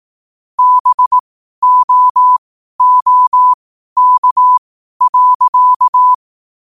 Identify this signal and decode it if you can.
Morse code, heard as a keyed tone, about 18 wpm, spaced as for 12 wpm, 1000 Hz. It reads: BOOK.
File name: qotdfiles / 12wpm